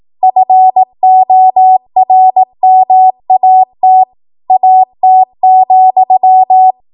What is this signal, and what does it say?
Morse code, heard as a keyed tone, about 18 wpm, 750 Hz. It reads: FORMAT AT,
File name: Morse / 18WPM